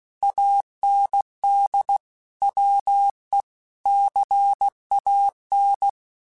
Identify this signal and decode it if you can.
Morse code, heard as a keyed tone, about 16 wpm, 800 Hz. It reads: AND WE CAN
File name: Technical Subjects / Technical Subjects-mp3s / Sw Pwr Sup @15